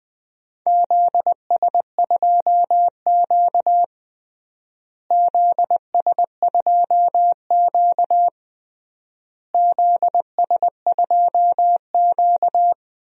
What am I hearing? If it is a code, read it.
Morse code, heard as a keyed tone, about 20 wpm, 700 Hz. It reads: ZS2Q ZS2Q ZS2Q
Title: DXCC-100 Call Signs - 4 Characters - Repeated 3x - 20wpm